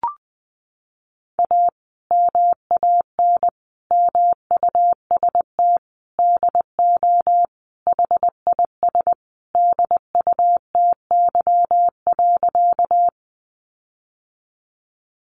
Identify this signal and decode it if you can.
Morse code, heard as a keyed tone, about 20 wpm, 700 Hz. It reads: A MAN MUST DO HIS DUTY.
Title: Sentences from Top 1500 Words 20wpm